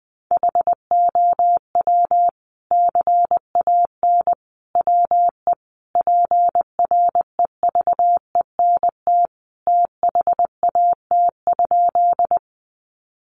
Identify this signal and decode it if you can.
Morse code, heard as a keyed tone, about 20 wpm, 700 Hz. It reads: HOW CAN WE PREVENT THAT?
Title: Sentences from Top 1100 Words 20wpm